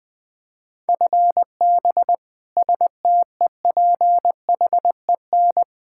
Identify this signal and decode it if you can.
Morse code, heard as a keyed tone, about 20 wpm, 700 Hz. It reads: FB STEPHEN
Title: Practice QSO - Exchange 3 - More Info - 20wpm